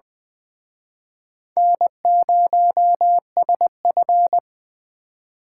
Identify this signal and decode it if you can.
Morse code, heard as a keyed tone, about 24 wpm, 700 Hz. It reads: N0SF
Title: Speed Racing - USA 1x2 Call Signs 20wpm